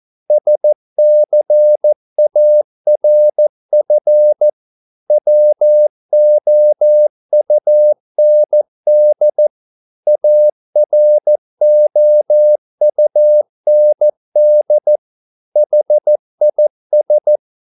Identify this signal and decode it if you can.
Morse code, heard as a keyed tone, about 14 wpm, 600 Hz. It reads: SCARF WOUND AROUND HIS